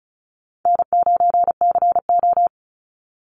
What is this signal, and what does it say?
Morse code, heard as a keyed tone, about 34 wpm, 700 Hz. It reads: N9CO
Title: Easy Call Signs Repeated 2x - 35wpm